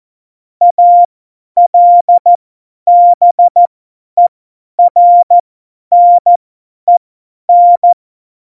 Morse code, heard as a keyed tone, about 14 wpm, spaced as for 10 wpm, 700 Hz. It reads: ALBERNEN
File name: Morsekurs / Witze / Witz 15. Dezember 2020, 50 F70